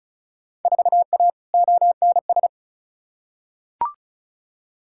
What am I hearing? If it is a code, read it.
Morse code, heard as a keyed tone, about 36 wpm, 700 Hz. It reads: 4A ONS
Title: Field Day Exchanges - 35wpm